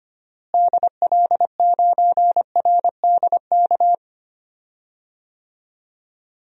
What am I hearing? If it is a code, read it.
Morse code, heard as a keyed tone, about 25 wpm, 700 Hz. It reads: DL9RDK
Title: DXCC-100 Call Signs - 6 Characters - Repeated 3x - 25wpm